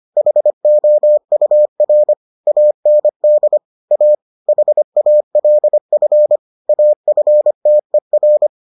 Morse code, heard as a keyed tone, about 25 wpm, 600 Hz. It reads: HOUR AND A HALF AFTER